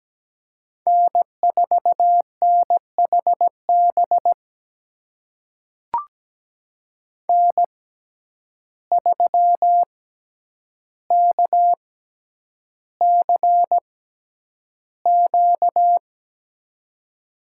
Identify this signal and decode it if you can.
Morse code, heard as a keyed tone, about 17 wpm, spaced as for 9 wpm, 700 Hz. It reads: N4NHB N3KCQ
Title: USA Call Signs - 5 Characters - 6x Character Spacing - 17wpm